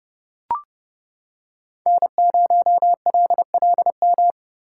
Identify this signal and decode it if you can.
Morse code, heard as a keyed tone, about 29 wpm, 700 Hz. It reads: N0LLM